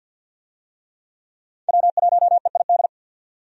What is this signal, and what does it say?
Morse code, heard as a keyed tone, about 50 wpm, 700 Hz. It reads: W1EID
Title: USA Call Signs - 5 Characters - 50wpm